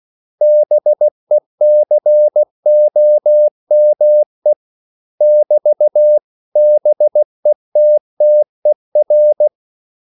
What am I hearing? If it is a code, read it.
Morse code, heard as a keyed tone, about 16 wpm, 600 Hz. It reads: BECOME <BT>BETTER
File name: CW / Practice / 500MostCommonEnglishWords16